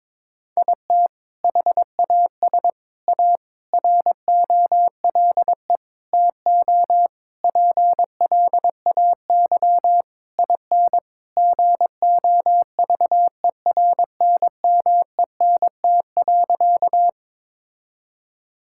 Morse code, heard as a keyed tone, about 22 wpm, 700 Hz. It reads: IT HAS A ROLE TO PLAY IN GOVERNMENT.
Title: Sentences from Top 1000 Words - Review Part 6 - 22wpm